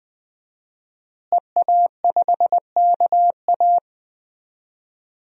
Morse code, heard as a keyed tone, about 21 wpm, 700 Hz. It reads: EA5KA
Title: DXCC-100 Call Signs - 5 Characters - Repeated 3x - 20wpm